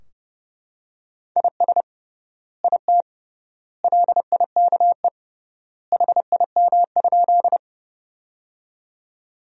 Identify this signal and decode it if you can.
Morse code, heard as a keyed tone, about 30 wpm, 700 Hz. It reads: IS IT LIKE HIM?
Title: 3x Word-Spacing - Sentences from Top 100 Words - 30wpm